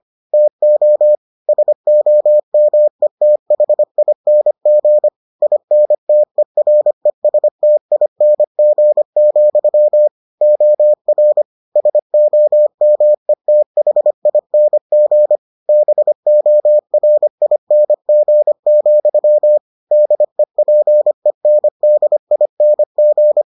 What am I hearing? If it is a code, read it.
Morse code, heard as a keyed tone, about 25 wpm, 600 Hz. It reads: TO SOMETHING INTERESTING, OR SOMETHING BORING, DEPENDING